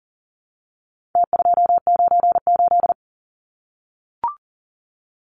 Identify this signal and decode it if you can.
Morse code, heard as a keyed tone, about 40 wpm, 700 Hz. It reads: T298
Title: Numbers - Contest Style - 4-Digit 40wpm